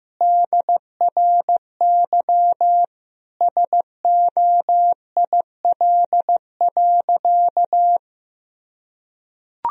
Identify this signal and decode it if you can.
Morse code, heard as a keyed tone, about 16 wpm, 700 Hz. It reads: DRY SOIL.
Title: Sentences from Top 1000 Words - Review Part 4 - 15wpm